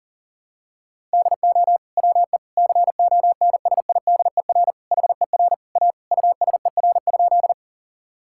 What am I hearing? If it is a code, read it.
Morse code, heard as a keyed tone, about 40 wpm, 700 Hz. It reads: DO WE CONSIDER HER A USER?
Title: Sentences from Top 1000 Words - Review Part 5 - 40wpm